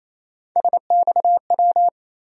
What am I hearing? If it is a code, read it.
Morse code, heard as a keyed tone, about 28 wpm, 700 Hz. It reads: SXW
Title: USA Call Sign Postfixes - 3 Letters 28wpm